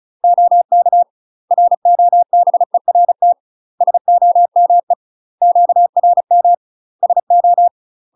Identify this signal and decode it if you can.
Morse code, heard as a keyed tone, about 35 wpm, 700 Hz. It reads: OK ROBERT SOME QRM SO
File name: UK MASTER QSOs TRAINING / 35WPM / UK MASTER QSOs 0025